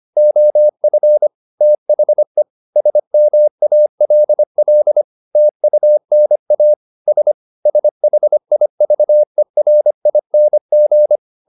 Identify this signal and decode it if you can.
Morse code, heard as a keyed tone, about 25 wpm, 600 Hz. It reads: OF THE SMALL TUNA S SHIVERING
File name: the old man and the sea part 1 25wpm